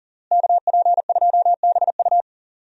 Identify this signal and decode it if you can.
Morse code, heard as a keyed tone, about 40 wpm, 700 Hz. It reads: KP2BU
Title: USA Call Signs - 5 Characters - Repeated 3x - 40wpm